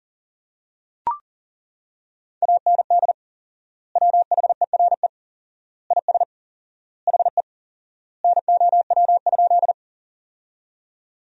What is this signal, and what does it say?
Morse code, heard as a keyed tone, about 40 wpm, 700 Hz. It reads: AND WHERE IS HE NOW?
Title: 4x Word-Spacing - Sentences from Top 200 Words - 40wpm